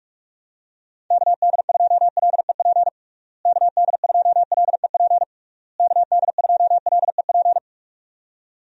Morse code, heard as a keyed tone, about 41 wpm, 700 Hz. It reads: KD2LEP KD2LEP KD2LEP
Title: USA Call Signs - 6 Characters - Repeated 3x - 45wpm